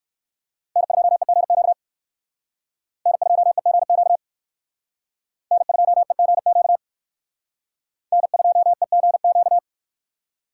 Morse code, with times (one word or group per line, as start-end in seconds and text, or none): 0.8-1.7 N2EGQ
3.1-4.2 N2EGQ
5.5-6.8 N2EGQ
8.1-9.6 N2EGQ